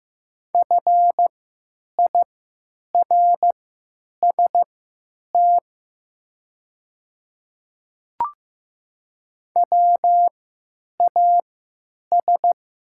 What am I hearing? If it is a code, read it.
Morse code, heard as a keyed tone, about 15 wpm, spaced as for 8 wpm, 700 Hz. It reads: FIRST WAS
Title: 3x Character Spacing - Top 100 Words - 15wpm